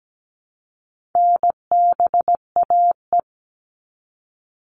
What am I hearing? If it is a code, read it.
Morse code, heard as a keyed tone, about 17 wpm, 700 Hz. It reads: NBAE
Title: All Cut Numbers - 4-Digit 17wpm